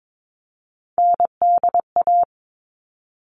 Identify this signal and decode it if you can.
Morse code, heard as a keyed tone, about 22 wpm, 700 Hz. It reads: NDA
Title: All Cut Numbers - 3-Digit 22wpm